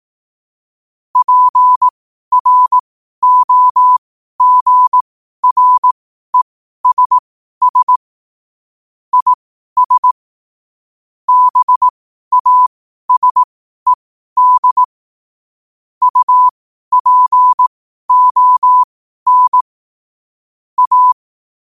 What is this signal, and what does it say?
Morse code, heard as a keyed tone, about 18 wpm, spaced as for 12 wpm, 1000 Hz. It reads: PROGRESS IS BASED UPON A
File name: qotdfiles / 12wpm